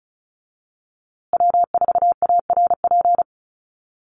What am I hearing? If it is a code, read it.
Morse code, heard as a keyed tone, about 34 wpm, 700 Hz. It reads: W4ARP
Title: USA Call Signs Repeated 3x 35wpm